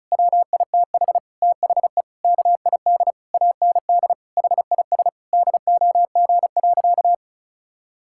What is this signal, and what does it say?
Morse code, heard as a keyed tone, about 35 wpm, 700 Hz. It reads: WITH THE KID AND HIS DOG.
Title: Sentences from Top 1600 Words 35wpm